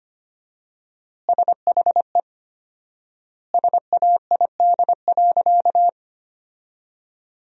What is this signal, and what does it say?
Morse code, heard as a keyed tone, about 25 wpm, 700 Hz. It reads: SHE SAID.
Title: 4x Word-Spacing - Sentences from Top 400 Words - 25wpm